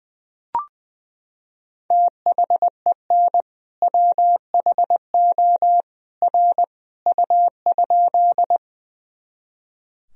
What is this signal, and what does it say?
Morse code, heard as a keyed tone, about 20 wpm, 700 Hz. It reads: THEN WHO R U?
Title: Sentences from Top 100 Words and Abbreviations 20wpm